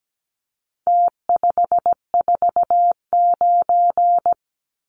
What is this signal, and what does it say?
Morse code, heard as a keyed tone, about 17 wpm, 700 Hz. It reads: T549
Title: Numbers - Contest Style - 4-Digit 17wpm